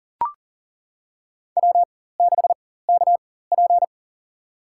Morse code, heard as a keyed tone, about 40 wpm, spaced as for 19 wpm, 700 Hz. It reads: W6KP
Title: USA Call Signs - 4 Characters - 4x Character Spacing - 40wpm